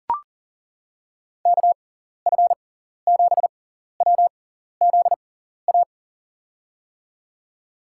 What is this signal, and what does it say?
Morse code, heard as a keyed tone, about 40 wpm, spaced as for 14 wpm, 700 Hz. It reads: KF7WZA
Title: USA Call Signs - 6 Characters - 6x Character Spacing - 40wpm